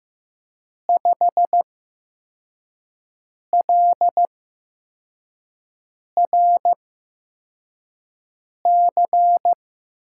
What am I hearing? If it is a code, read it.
Morse code, heard as a keyed tone, about 15 wpm, spaced as for 4 wpm, 700 Hz. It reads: 5LRC
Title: USA Call Signs - 6 Characters - 8x Character Spacing - 15wpm